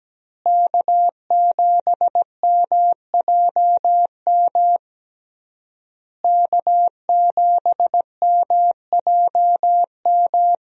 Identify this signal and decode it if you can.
Morse code, heard as a keyed tone, about 17 wpm, 700 Hz. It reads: K7MJM K7MJM
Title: USA Call Signs - 5 Characters - Repeated 3x - 17wpm